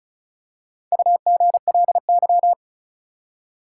Morse code, heard as a keyed tone, about 36 wpm, 700 Hz. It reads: UGLY
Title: Four Letter Words - 35wpm